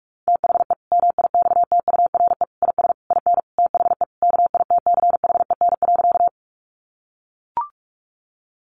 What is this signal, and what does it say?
Morse code, heard as a keyed tone, about 44 wpm, 700 Hz. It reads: THE MIXTURE IS IN THE KITCHEN.
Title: Sentences from Top 1500 Words 45wpm